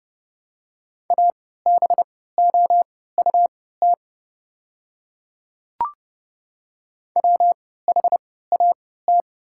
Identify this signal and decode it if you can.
Morse code, heard as a keyed tone, about 30 wpm, spaced as for 17 wpm, 700 Hz. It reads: ABOUT WHAT
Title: 3x Character Spacing - Top 100 Words - 30wpm